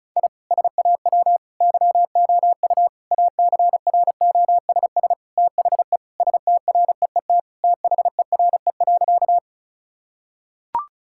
Morse code, heard as a keyed tone, about 36 wpm, 700 Hz. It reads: I SAW YOU ACROSS THE STREET THERE.
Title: Sentences from Top 1000 Words - Review Part 5 - 35wpm